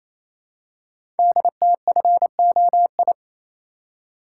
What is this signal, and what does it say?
Morse code, heard as a keyed tone, about 29 wpm, 700 Hz. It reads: DTFOI
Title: Code Groups - 5 Letters - 28wpm